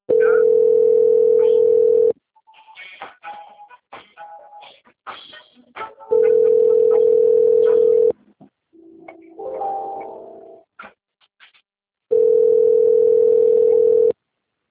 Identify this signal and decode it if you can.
Ringback tone (North America), 440 + 480 Hz.